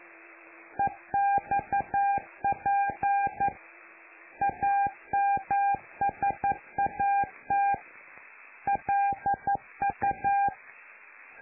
Morse code, heard as a keyed tone, about 11 wpm, 800 Hz. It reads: EXP AMSAT LU
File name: lu4aao / Globo Amsat